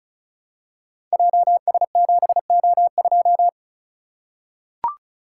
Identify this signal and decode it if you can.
Morse code, heard as a keyed tone, about 36 wpm, 700 Hz. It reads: JS7O2
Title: Code Groups - 5 Letter:Numbers - 35wpm